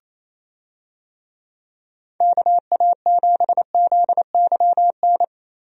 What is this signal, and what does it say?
Morse code, heard as a keyed tone, about 28 wpm, 700 Hz. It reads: KA7ZYN
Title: USA Call Signs - 6 Characters - Repeated 2x - 28wpm